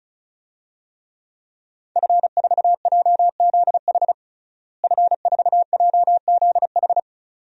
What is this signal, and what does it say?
Morse code, heard as a keyed tone, about 36 wpm, 700 Hz. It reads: F4JZH F4JZH
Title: DXCC-100 Call Signs - 5 Characters - Repeated 2x - 35wpm